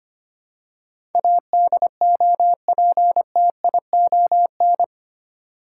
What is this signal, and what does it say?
Morse code, heard as a keyed tone, about 25 wpm, 700 Hz. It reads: ADOPTION
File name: 8 Letter Words - First 3 Letters Given - 25wpm